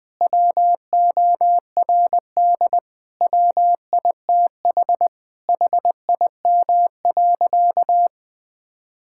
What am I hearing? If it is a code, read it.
Morse code, heard as a keyed tone, about 20 wpm, 700 Hz. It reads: WORD WITH HIM.
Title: Sentences from Top 1000 Words - Review Part 6 - 20wpm